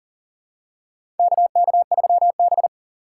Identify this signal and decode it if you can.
Morse code, heard as a keyed tone, about 40 wpm, 700 Hz. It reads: KK3B
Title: USA Call Signs - 4 Characters - 4x Character Spacing - 40wpm